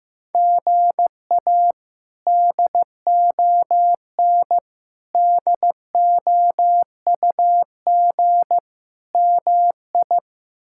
Morse code, heard as a keyed tone, about 15 wpm, 700 Hz. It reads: GA DON DOUG MI
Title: K1USN SST QSO Practice - 15wpm